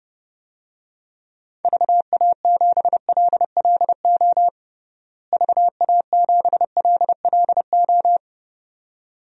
Morse code, heard as a keyed tone, about 30 wpm, 700 Hz. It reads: VA7LLO VA7LLO
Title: DXCC-100 Call Signs - 6 Characters - Repeated 2x - 30wpm